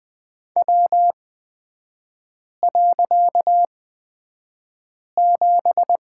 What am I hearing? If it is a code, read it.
Morse code, heard as a keyed tone, about 20 wpm, spaced as for 5 wpm, 690 Hz. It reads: W.7
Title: 255 Lesson 41e - Period - Three Characters 20wpm